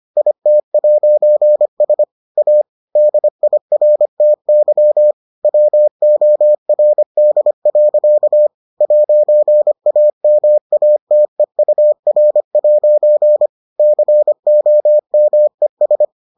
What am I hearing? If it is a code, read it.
Morse code, heard as a keyed tone, about 25 wpm, 600 Hz. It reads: IT'S A DIRTY WORD. 'AMATEUR' COMES